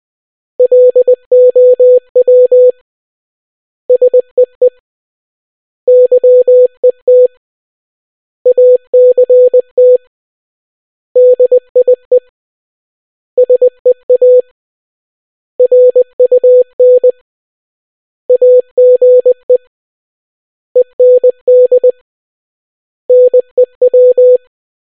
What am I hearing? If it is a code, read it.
Morse code, heard as a keyed tone, about 20 wpm, 500 Hz. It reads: LOW SEE YET ACT DIE SEA RUN AGE END NEW